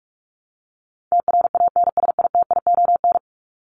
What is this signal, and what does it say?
Morse code, heard as a keyed tone, about 44 wpm, 700 Hz. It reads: TRANSITION
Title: Top 1751-2000 Words 45wpm